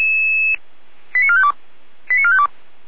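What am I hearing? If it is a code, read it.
Five-tone selective call (ZVEI), heard as quick runs of five tones, tones of about 70 ms. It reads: 89452 89452
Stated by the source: ZVEI five-tone selective call 89451 received on an SDR.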